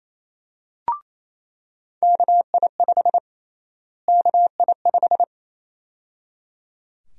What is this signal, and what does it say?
Morse code, heard as a keyed tone, about 28 wpm, 700 Hz. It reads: KI5 KI5
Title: USA Call Sign Prefixes - Repeated 2x - 28wpm